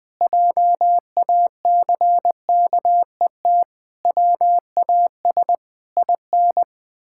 Morse code, heard as a keyed tone, about 21 wpm, 700 Hz. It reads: JACKET WAS IN
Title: Pangram Challenge v2 - 20wpm